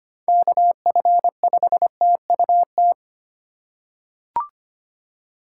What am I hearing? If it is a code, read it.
Morse code, heard as a keyed tone, about 25 wpm, 690 Hz. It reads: KF5TUT
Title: USA Call Signs - 6 Characters - 2x Character Spacing - 25wpm